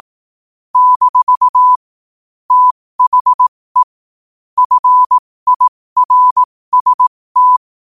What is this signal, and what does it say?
Morse code, heard as a keyed tone, about 18 wpm, spaced as for 15 wpm, 1000 Hz. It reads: - THE FIRST
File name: qotdfiles / 15wpm